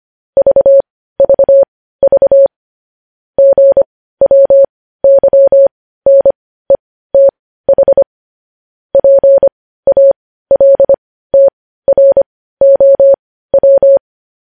Morse code, heard as a keyed tone, about 25 wpm, spaced as for 15 wpm, 550 Hz. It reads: VVV GWYNETH PALTROW
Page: news headlines in morse code at 15 wpm